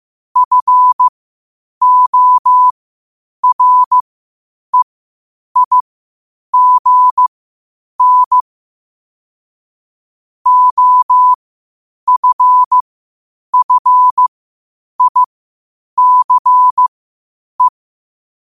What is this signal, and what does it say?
Morse code, heard as a keyed tone, about 15 wpm, spaced as for 8 wpm, 1000 Hz. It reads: FOREIGN OFFICE
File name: qotdfiles / 8wpm